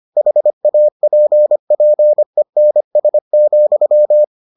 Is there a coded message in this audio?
Yes: Morse code from 0.2 to 4.2 s, heard as a keyed tone, about 25 wpm, 600 Hz. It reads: HAPPENS,